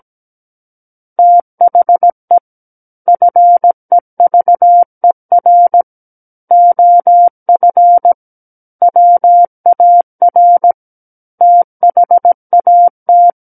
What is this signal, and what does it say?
Morse code, heard as a keyed tone, about 17 wpm, 700 Hz. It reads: THE FEVER OF WAR THAT
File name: Book One / 08-Chapter 8 - 17 WPM